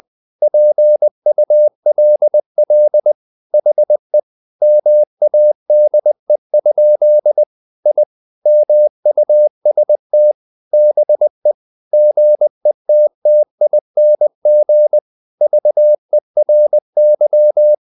Morse code, heard as a keyed tone, about 20 wpm, 600 Hz. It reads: PULL HE MADE? I MUST BE GETTING VERY
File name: the old man and the sea / the old man and the sea part 1 20wpm